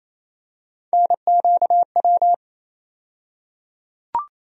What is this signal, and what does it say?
Morse code, heard as a keyed tone, about 28 wpm, 700 Hz. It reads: NQW
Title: USA Call Sign Postfixes - 3 Letters 28wpm